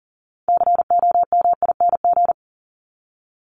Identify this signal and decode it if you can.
Morse code, heard as a keyed tone, about 40 wpm, 700 Hz. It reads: COMING
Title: ING Suffix - Top 100 Words with Base Words 40wpm